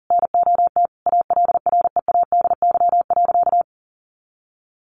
Morse code, heard as a keyed tone, about 40 wpm, 700 Hz. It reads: NOT ALREADY.
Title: Sentences from Top 900 Words 40wpm